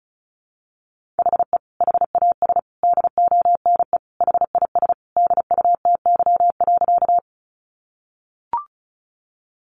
Morse code, heard as a keyed tone, about 34 wpm, 700 Hz. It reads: HE HAS DONE HIS DUTY.
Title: Sentences from Top 1500 Words 35wpm